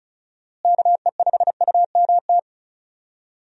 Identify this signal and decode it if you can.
Morse code, heard as a keyed tone, about 36 wpm, 700 Hz. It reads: KE5UMT
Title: USA Call Signs - 6 Characters - Repeated 2x - 35wpm